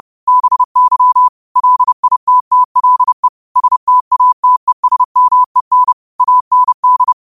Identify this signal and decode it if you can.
Morse code, heard as a keyed tone, about 30 wpm, 1000 Hz. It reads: BY LITTLE STATESMEN AND
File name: qotdfiles / 30wpm